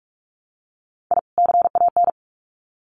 Morse code, heard as a keyed tone, about 44 wpm, 700 Hz. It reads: I CAN
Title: Sets of 2 Words 45wpm